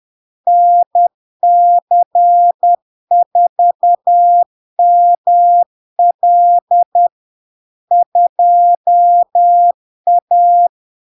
Morse code, heard as a keyed tone, about 10 wpm, 700 Hz. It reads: NC4ML 2A
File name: LARK-FD-2022-Log-in-CW-10wpm-700-Hz